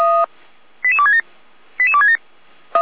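Five-tone selective call (ZVEI), heard as quick runs of five tones, tones of about 70 ms. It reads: 88267 88267
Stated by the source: ZVEI five-tone selective call received on an SDR, fire brigade alerting.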